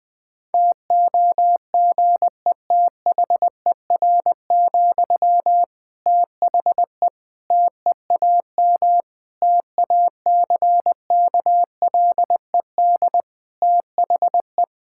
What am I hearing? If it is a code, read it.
Morse code, heard as a keyed tone, about 20 wpm, 700 Hz. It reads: TOGETHER, THE TEAM TACKLED THE
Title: Difficult Sentences with Hints - 20wpm